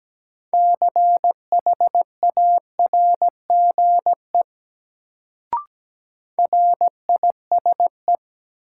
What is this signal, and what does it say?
Morse code, heard as a keyed tone, about 17 wpm, 700 Hz. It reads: CHARGE RISE
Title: Sentences from Top 900 Words - New Word Drill - 17wpm